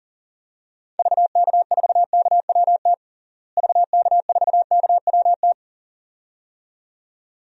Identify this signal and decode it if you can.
Morse code, heard as a keyed tone, about 40 wpm, 700 Hz. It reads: VK4KWT VK4KWT
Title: DXCC-100 Call Signs - 6 Characters - Repeated 2x - 40wpm